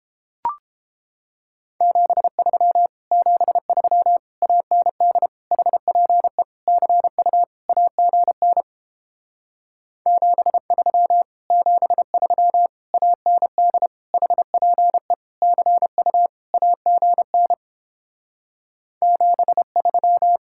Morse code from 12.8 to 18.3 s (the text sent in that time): AND HPE CU AGN